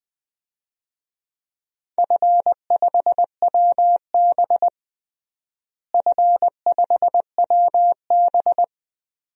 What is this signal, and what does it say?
Morse code, heard as a keyed tone, about 21 wpm, 700 Hz. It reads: F5WB F5WB
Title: DXCC-100 Call Signs - 4 Characters - Repeated 2x - 20wpm